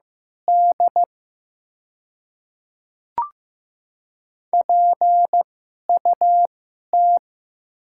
Morse code, heard as a keyed tone, about 15 wpm, spaced as for 11 wpm, 700 Hz. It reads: D PUT
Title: 2x Character Spacing - Top 500 Words - 15wpm